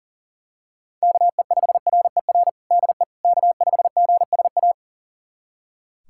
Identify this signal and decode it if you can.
Morse code, heard as a keyed tone, about 40 wpm, 700 Hz. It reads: KE5RER DE K5GSA
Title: USA Callsign DE Callsign 40wpm